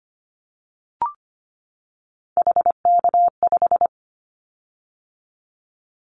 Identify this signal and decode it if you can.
Morse code, heard as a keyed tone, about 25 wpm, 700 Hz. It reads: HK5
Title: Callsign Prefixes for Top 100 DXCC - 25wpm